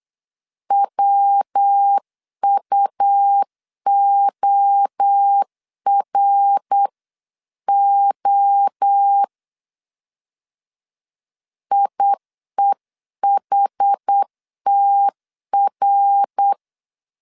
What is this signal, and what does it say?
Morse code, heard as a keyed tone, about 8 wpm, 800 Hz. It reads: WUORO IEHTR